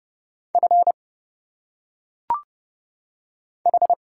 Morse code, heard as a keyed tone, about 30 wpm, 700 Hz. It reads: F H
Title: Single Letters - Phonetic Pronunciation 30wpm